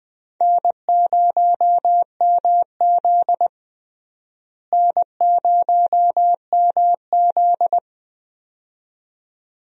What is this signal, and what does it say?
Morse code, heard as a keyed tone, about 20 wpm, 700 Hz. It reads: N0MZ N0MZ
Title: USA Call Signs - 4 Characters - Repeated 2x - 20wpm